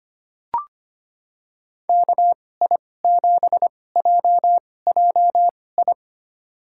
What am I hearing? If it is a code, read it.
Morse code, heard as a keyed tone, about 25 wpm, spaced as for 18 wpm, 700 Hz. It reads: KI7JJI
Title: USA Call Signs - 2x Character Spacing - 25wpm